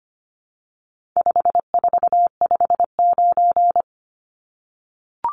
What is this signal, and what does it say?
Morse code, heard as a keyed tone, about 25 wpm, 700 Hz. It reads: 5459
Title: Numbers - 4-Digit 25wpm